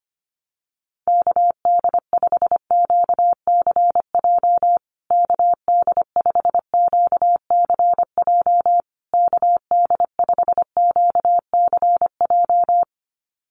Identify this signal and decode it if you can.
Morse code, heard as a keyed tone, about 25 wpm, 700 Hz. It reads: KD5QCJ KD5QCJ KD5QCJ
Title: USA Call Signs Repeated 3x 25wpm